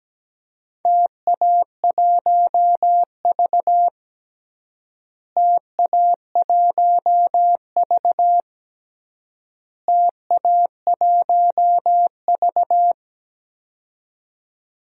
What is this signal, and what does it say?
Morse code, heard as a keyed tone, about 17 wpm, 700 Hz. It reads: TA1V TA1V TA1V
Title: DXCC-100 Call Signs - 4 Characters - Repeated 3x - 17wpm